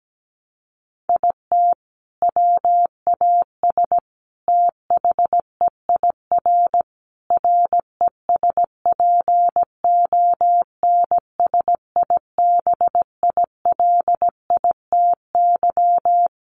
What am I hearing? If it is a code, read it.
Morse code, heard as a keyed tone, about 17 wpm, 700 Hz. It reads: IT WAS THEIR RESPONSIBILITY
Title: Sentences from Top 1200 Words 17wpm